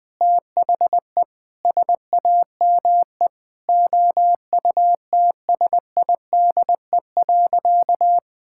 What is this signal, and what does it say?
Morse code, heard as a keyed tone, about 20 wpm, 700 Hz. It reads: THE SAME OUTSIDE.
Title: Sentences from Top 1000 Words - Review Part 6 - 20wpm